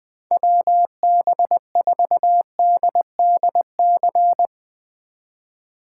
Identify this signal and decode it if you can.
Morse code, heard as a keyed tone, about 20 wpm, 700 Hz. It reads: WB4DDC
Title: Speed Racing - USA 2x3 Call Signs 20wpm